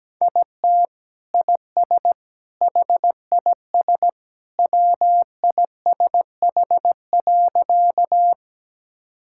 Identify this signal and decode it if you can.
Morse code, heard as a keyed tone, about 17 wpm, 700 Hz. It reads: IT IS HIS WISH.
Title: Sentences from Top 1600 Words 17wpm